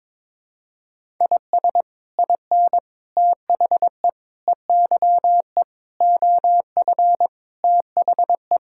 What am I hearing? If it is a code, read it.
Morse code, heard as a keyed tone, about 22 wpm, 700 Hz. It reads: IS IN THE EYE OF THE
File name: practice / 22wpm / Sets of 7 Words - 22wpm